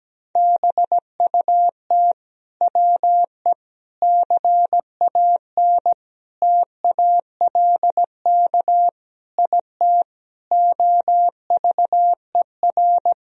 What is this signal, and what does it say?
Morse code, heard as a keyed tone, about 17 wpm, 700 Hz. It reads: BUT WE CAN TALK IT OVER
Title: Sentences from Top 1000 Words - Review Part 4 - 17wpm